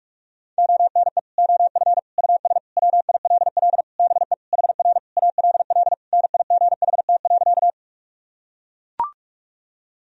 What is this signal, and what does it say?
Morse code, heard as a keyed tone, about 41 wpm, 700 Hz. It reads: ONE OF US WILL BE HR ALL NIGHT.
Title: Sentences from Top 300 Words and Abbreviations 45wpm